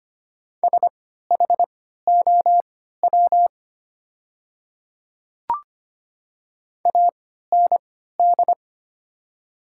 Morse code, heard as a keyed tone, about 25 wpm, spaced as for 14 wpm, 700 Hz. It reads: SHOW AND
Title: 3x Character Spacing - Top 200 Words - 25wpm